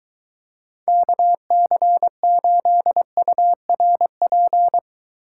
Morse code, heard as a keyed tone, about 23 wpm, 700 Hz. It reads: KC8URP